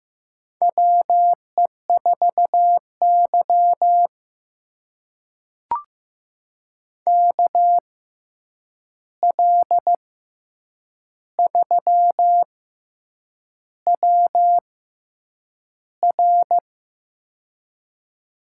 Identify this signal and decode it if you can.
Morse code, heard as a keyed tone, about 15 wpm, spaced as for 7 wpm, 700 Hz. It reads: WE4Y KL3WR
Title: USA Call Signs - 6x Character Spacing - 15wpm